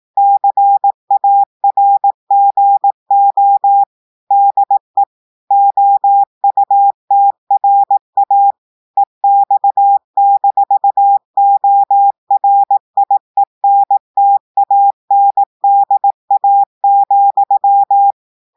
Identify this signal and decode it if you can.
Morse code, heard as a keyed tone, about 18 wpm, 800 Hz. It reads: CARGO DE OUTRA EX-ORIENTANDA,
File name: audios / portugais / portugais texte en clair 1 vitesse 18 mots minutes